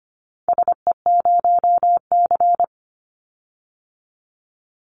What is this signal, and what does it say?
Morse code, heard as a keyed tone, about 25 wpm, 700 Hz. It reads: SE0C
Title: Call Signs Repeated 3x - 25wpm